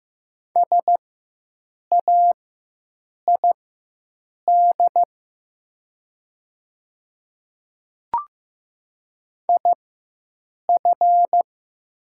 Morse code, heard as a keyed tone, about 15 wpm, spaced as for 7 wpm, 700 Hz. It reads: SAID IF
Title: 4x Character Spacing - Top 100 Words - 15wpm